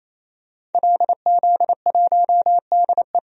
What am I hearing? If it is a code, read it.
Morse code, heard as a keyed tone, about 27 wpm, 700 Hz. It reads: LZ1DE
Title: DXCC-100 Call Signs - 5 Characters - 28wpm